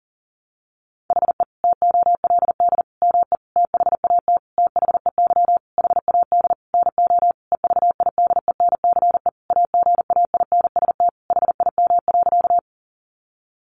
Morse code, heard as a keyed tone, about 40 wpm, 700 Hz. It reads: HE TOLD ME THAT THEY HAD NO EVIDENCE AGAINST HIM.
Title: Sentences from Top 1300 Words 40wpm